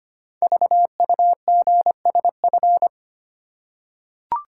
Morse code, heard as a keyed tone, about 25 wpm, 700 Hz. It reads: VUGSF
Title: Code Groups - 5 Letters - 25wpm